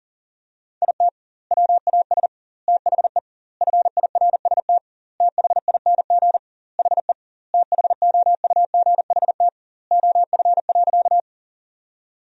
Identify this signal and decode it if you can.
Morse code, heard as a keyed tone, about 40 wpm, 700 Hz. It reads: IT WAS THE FIRST THING HE THOUGHT OF.
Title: 2x Word-Spacing - Sentences from Top 300 Words - 40wpm